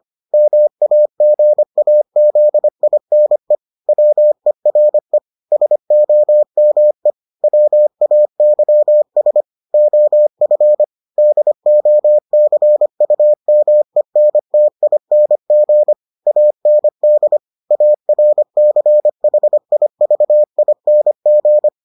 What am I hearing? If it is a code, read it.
Morse code, heard as a keyed tone, about 25 wpm, 600 Hz. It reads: MAGAZINE WERE SOME WAYS OF DOCUMENTING AND ARCHIVING